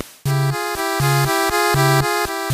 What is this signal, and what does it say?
There is no code.